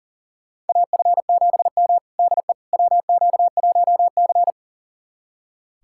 Morse code, heard as a keyed tone, about 37 wpm, 700 Hz. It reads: AF7M DE WQ1C
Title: USA Callsign DE Callsign - 4 Characters - 40wpm